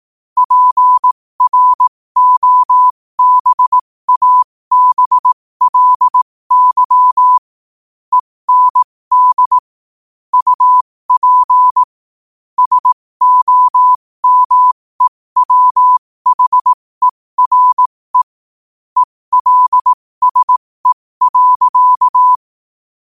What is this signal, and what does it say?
Morse code, heard as a keyed tone, about 18 wpm, spaced as for 15 wpm, 1000 Hz. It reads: PROBABLY END UP SOMEWHERE ELSE.
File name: qotdfiles / 15wpm